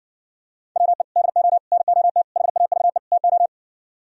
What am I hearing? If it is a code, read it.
Morse code, heard as a keyed tone, about 59 wpm, 700 Hz. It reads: WE DO NOT HAVE TO